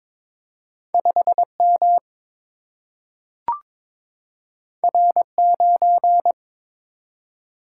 Morse code, heard as a keyed tone, about 22 wpm, 700 Hz. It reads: 5 M R 9
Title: Single-Letter Single-Number 22wpm